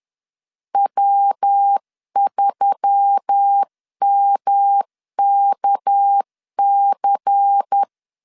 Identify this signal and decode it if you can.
Morse code, heard as a keyed tone, about 10 wpm, 800 Hz. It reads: W3MKC
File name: CW / DET / 1 6 1500C S C 10wpm 4252